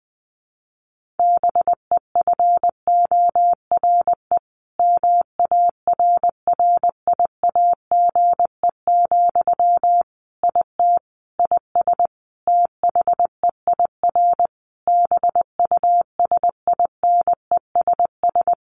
Morse code, heard as a keyed tone, about 19 wpm, 700 Hz. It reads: BEFORE MARRIAGE, IT IS THEIR BUSINESS
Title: Sentences from Top 1500 Words 20wpm